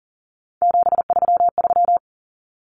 Morse code, heard as a keyed tone, about 40 wpm, 700 Hz. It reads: 733
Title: Common Cut Numbers - 3-Digit 40wpm